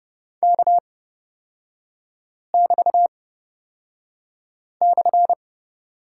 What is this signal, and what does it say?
Morse code, heard as a keyed tone, about 30 wpm, spaced as for 5 wpm, 700 Hz. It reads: K<BT>/
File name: Lesson 36e - BT - Three Characters 30wpm